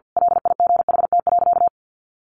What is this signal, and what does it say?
Morse code, heard as a keyed tone, about 50 wpm, 700 Hz. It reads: LIGHT.
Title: A Sound Of Thunder 50wpm